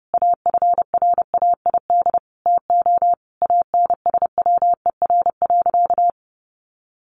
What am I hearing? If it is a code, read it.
Morse code, heard as a keyed tone, about 30 wpm, 700 Hz. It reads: AFRAID TO ANSWER.
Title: Sentences from Top 1500 Words 30wpm